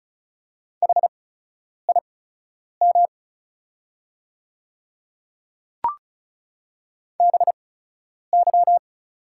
Morse code, heard as a keyed tone, about 36 wpm, spaced as for 10 wpm, 690 Hz. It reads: HIM BY
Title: 8x Character Spacing - Top 200 Words - 35wpm